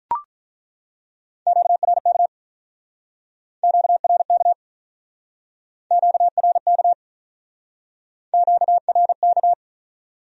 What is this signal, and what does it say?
Morse code, heard as a keyed tone, about 48 wpm, 700 Hz. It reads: QRK QRK QRK QRK